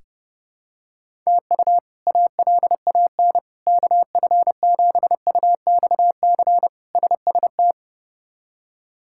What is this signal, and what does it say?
Morse code, heard as a keyed tone, about 30 wpm, 700 Hz. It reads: TU ALAN KF7UXC SST
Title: K1USN SST QSO Practice - 30wpm